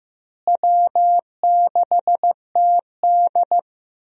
Morse code, heard as a keyed tone, about 15 wpm, 700 Hz. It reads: W6TD
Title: USA Call Signs - 4 Characters - Repeated 2x - 15wpm